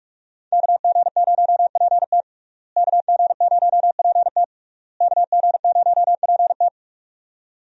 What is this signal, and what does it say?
Morse code, heard as a keyed tone, about 46 wpm, 700 Hz. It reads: KG0PT KG0PT KG0PT
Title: USA Call Signs - 5 Characters - Repeated 3x - 45wpm